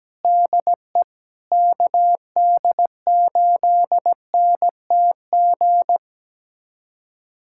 Morse code, heard as a keyed tone, about 17 wpm, 700 Hz. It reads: DE KD8NTG
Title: USA Callsign DE Callsign - 6 Characters - 17wpm